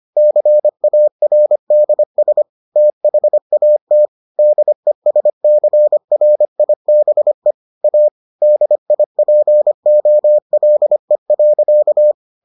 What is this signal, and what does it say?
Morse code, heard as a keyed tone, about 25 wpm, 600 Hz. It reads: CARDS THAT DESCRIBE A DIPOLE.